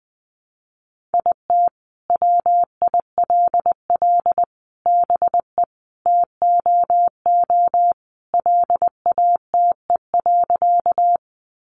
Morse code, heard as a keyed tone, about 20 wpm, 700 Hz. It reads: IT WILL BE TOO LATE.
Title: Sentences from Top 300 Words 20wpm